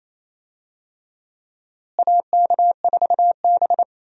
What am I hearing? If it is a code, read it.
Morse code, heard as a keyed tone, about 28 wpm, 700 Hz. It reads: AK4B
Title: USA Call Signs - 4 Characters - Repeated 3x - 28wpm